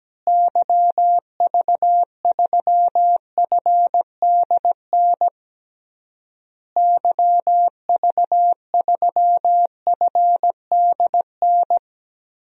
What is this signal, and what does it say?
Morse code, heard as a keyed tone, about 17 wpm, 700 Hz. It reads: YV3FDN YV3FDN
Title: DXCC-100 Call Signs - 6 Characters - Repeated 3x - 17wpm